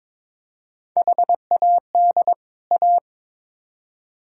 Morse code, heard as a keyed tone, about 22 wpm, 700 Hz. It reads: HAD A
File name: Build A Phrase - 2-Word Phrases - 22wpm